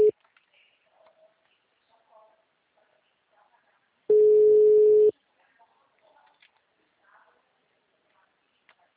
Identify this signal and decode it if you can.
Ringback tone (Europe, 425 Hz), with speech.
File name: out- -2004- - - .1674